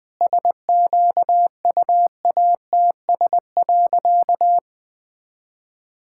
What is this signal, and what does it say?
Morse code, heard as a keyed tone, about 20 wpm, 700 Hz. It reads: SQUATS.
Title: All-Age Appropriate Jokes - 20wpm